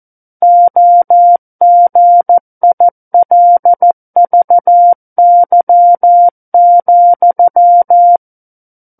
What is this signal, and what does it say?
Morse code, heard as a keyed tone, about 14 wpm, 700 Hz. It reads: OGILVY,
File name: Book One / 05-Chapter 5 - 14 WPM